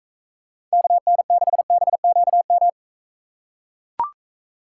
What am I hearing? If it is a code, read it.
Morse code, heard as a keyed tone, about 44 wpm, 700 Hz. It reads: KN6BQM